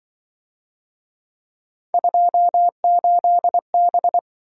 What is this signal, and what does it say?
Morse code, heard as a keyed tone, about 26 wpm, 700 Hz. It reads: 28B